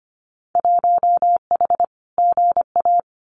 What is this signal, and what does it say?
Morse code, heard as a keyed tone, about 25 wpm, 700 Hz. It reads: 1H GA
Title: Winter Field Day Exchanges 25wpm